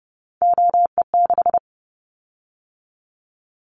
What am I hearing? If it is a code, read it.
Morse code, heard as a keyed tone, about 30 wpm, 700 Hz. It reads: OE6
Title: Callsign Prefixes for Top 100 DXCC - 30wpm